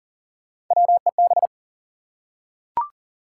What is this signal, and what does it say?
Morse code, heard as a keyed tone, about 40 wpm, 700 Hz. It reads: WEB
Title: Three Letter Words - 40wpm